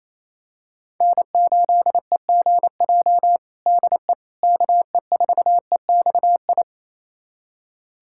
Morse code, heard as a keyed tone, about 28 wpm, 700 Hz. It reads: N8EGJ DE KE4EXI
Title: USA Callsign DE Callsign 28wpm